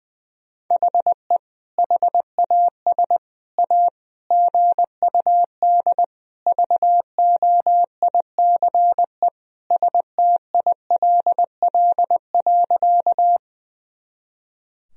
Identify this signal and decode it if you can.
Morse code, heard as a keyed tone, about 21 wpm, 700 Hz. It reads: HE HAS A GUD VOICE STILL.
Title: Sentences from Top 500 Words and Abbreviations 20wpm